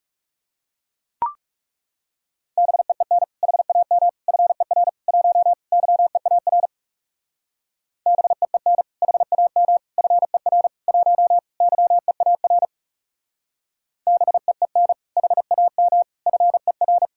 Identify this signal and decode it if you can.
Morse code, heard as a keyed tone, about 37 wpm, 700 Hz. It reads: BEEN HAM FER 1 YEAR BEEN HAM FER 1 YEAR BEEN HAM FER
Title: Speed-Racing Common QSO Phrases - 30wpm